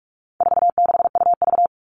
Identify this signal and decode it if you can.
Morse code, heard as a keyed tone, about 44 wpm, 700 Hz. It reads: 46UV
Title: All Cut Numbers - 4-Digit 45wpm